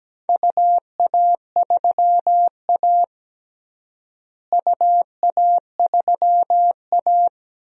Morse code, heard as a keyed tone, about 17 wpm, 700 Hz. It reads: UA3A UA3A
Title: DXCC-100 Call Signs - 4 Characters - Repeated 2x - 17wpm